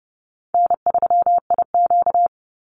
Morse code, heard as a keyed tone, about 30 wpm, 700 Hz. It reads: N3IQ
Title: Call Signs Repeated 2x - 30wpm